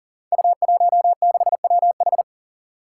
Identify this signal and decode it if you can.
Morse code, heard as a keyed tone, about 40 wpm, 700 Hz. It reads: U16WH
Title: Five Challenging Letter-Number Combinations 40wpm